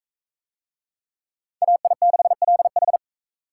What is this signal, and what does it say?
Morse code, heard as a keyed tone, about 38 wpm, 700 Hz. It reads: AI6LH